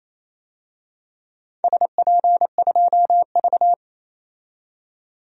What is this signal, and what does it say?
Morse code, heard as a keyed tone, about 28 wpm, 700 Hz. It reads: SP2V
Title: DXCC-100 Call Signs - 4 Characters - 28wpm